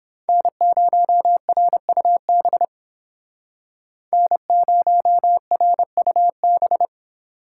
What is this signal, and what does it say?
Morse code, heard as a keyed tone, about 29 wpm, 700 Hz. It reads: N0RUB N0RUB